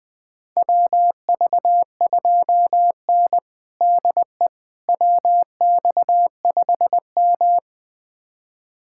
Morse code, heard as a keyed tone, about 20 wpm, 700 Hz. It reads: WV2N DE WX5M
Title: USA Callsign DE Callsign - 4 Characters - 20wpm